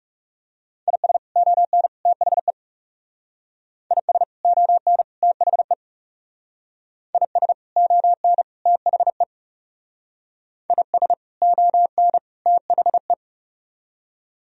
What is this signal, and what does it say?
Morse code, heard as a keyed tone, about 37 wpm, 700 Hz. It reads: IS ON THE IS ON THE IS ON THE IS ON THE
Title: Speed-Racing - Sets of 3 Words 30wpm